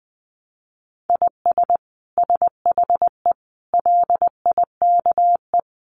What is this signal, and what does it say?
Morse code, heard as a keyed tone, about 20 wpm, 700 Hz. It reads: IS SHE LIKE
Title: Sentences from Top 100 Words 20wpm